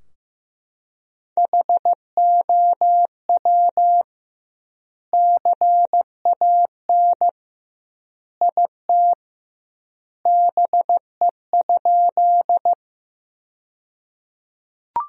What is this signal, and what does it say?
Morse code, heard as a keyed tone, about 15 wpm, 700 Hz. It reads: HOW CAN IT BE?
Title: 2x Word-Spacing - Sentences from Top 100 Words - 15wpm